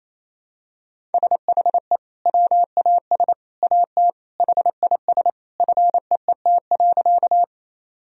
Morse code, heard as a keyed tone, about 27 wpm, 700 Hz. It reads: SHE WAS AT HIS FEET.
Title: Sentences from Top 1000 Words - Review Part 1 - 28wpm